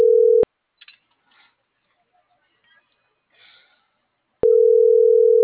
Ringback tone (North America), 440 + 480 Hz.